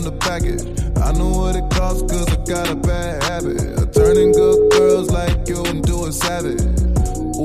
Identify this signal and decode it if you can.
Ringback tone (Europe, 425 Hz), with noise.